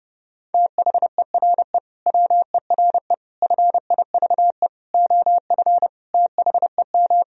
Morse code, heard as a keyed tone, about 30 wpm, 700 Hz. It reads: THERE WERE FIVE OF THEM
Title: Sentences from Top 1000 Words - Review Part 2 - 30wpm